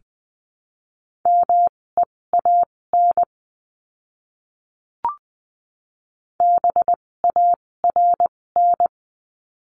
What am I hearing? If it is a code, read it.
Morse code, heard as a keyed tone, about 20 wpm, spaced as for 16 wpm, 700 Hz. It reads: MEAN BARN
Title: 256 Lesson 42a - Words 12-20wpm